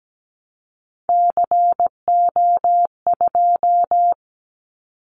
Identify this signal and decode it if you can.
Morse code, heard as a keyed tone, about 17 wpm, 700 Hz. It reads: CO2
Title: Callsign Prefixes for Top 100 DXCC - 17wpm